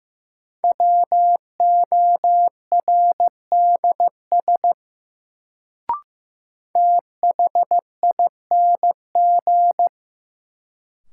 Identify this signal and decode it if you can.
Morse code, heard as a keyed tone, about 15 wpm, 700 Hz. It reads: WORDS THING
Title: Sets of 3 Words - New Word Drill - 15wpm